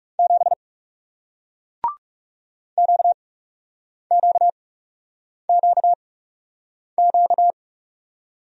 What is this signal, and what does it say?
Morse code, heard as a keyed tone, about 40 wpm, spaced as for 8 wpm, 700 Hz. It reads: 7 QQQQ